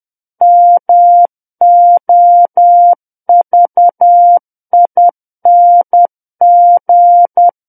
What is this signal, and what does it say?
Morse code, heard as a keyed tone, about 10 wpm, 700 Hz. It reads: MOVING
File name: Book One / 01-Chapter 1 - 10 WPM